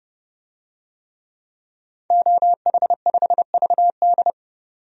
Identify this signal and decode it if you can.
Morse code, heard as a keyed tone, about 31 wpm, 700 Hz. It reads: OH5VD
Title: DXCC-100 Call Signs - 5 Characters - 30wpm